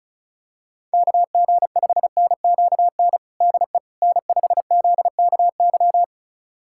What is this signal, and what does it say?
Morse code, heard as a keyed tone, about 34 wpm, 700 Hz. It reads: KG5NQN DE N5ZKY
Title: USA Callsign DE Callsign 35wpm